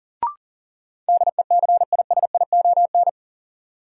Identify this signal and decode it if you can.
Morse code, heard as a keyed tone, about 38 wpm, 700 Hz. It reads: DECISION
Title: Sentences from Top 1000 Words - New Word Drill - 40wpm